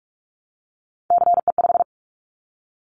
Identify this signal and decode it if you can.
Morse code, heard as a keyed tone, about 46 wpm, 700 Hz. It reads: CE5
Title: Callsign Prefixes for Top 100 DXCC - 45wpm